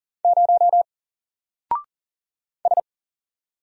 Morse code, heard as a keyed tone, about 40 wpm, 700 Hz. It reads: 0 S
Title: Single Character (Letter:Number:Punctuation:Prosign) 40wpm